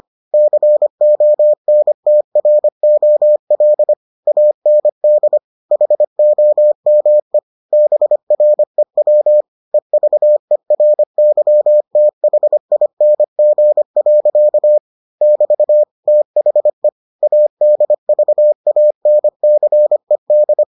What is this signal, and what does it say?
Morse code, heard as a keyed tone, about 25 wpm, 600 Hz. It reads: CONTROL AND HOME BREW EVERYTHING. <BT> THE ADVANCED